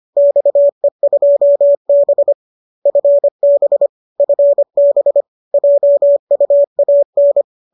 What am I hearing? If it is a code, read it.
Morse code, heard as a keyed tone, about 25 wpm, 590 Hz. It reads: XE2B FB FB JUAN
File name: qso204 25